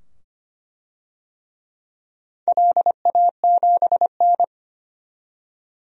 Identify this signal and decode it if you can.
Morse code, heard as a keyed tone, about 25 wpm, 700 Hz. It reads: LA7N
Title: DXCC-100 Call Signs - 4 Characters - 25wpm